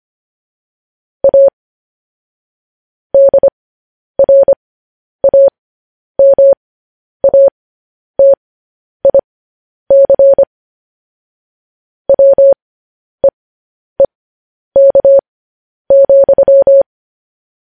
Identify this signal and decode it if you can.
Morse code, heard as a keyed tone, about 25 wpm, spaced as for 10 wpm, 550 Hz. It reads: A DRAMATIC WEEK,
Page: news headlines in morse code at 10 wpm